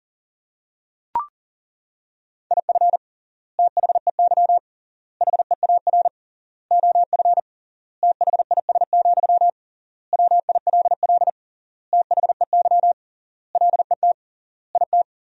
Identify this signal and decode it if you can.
Morse code, heard as a keyed tone, about 37 wpm, 700 Hz. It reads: IF THEY HEAR OF THIS, WILL THEY LET IT
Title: 3x Word-Spacing - Sentences from Top 400 Words - 40wpm